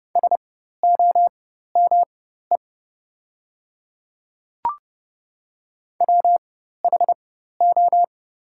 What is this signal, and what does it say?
Morse code, heard as a keyed tone, about 30 wpm, spaced as for 14 wpm, 690 Hz. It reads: SOME WHO
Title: 4x Character Spacing - Top 200 Words - 30wpm